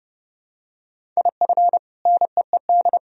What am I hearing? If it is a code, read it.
Morse code, heard as a keyed tone, about 30 wpm, 700 Hz. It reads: IF NEED
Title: Sentences from Top 1000 Words - Review Part 1 - 30wpm